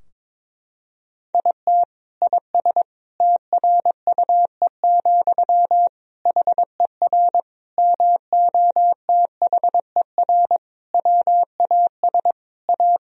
Morse code, heard as a keyed tone, about 21 wpm, 700 Hz. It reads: IT IS TRUE, HER MOTHER WAS A
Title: Sentences from Top 1000 Words - Review Part 2 - 22wpm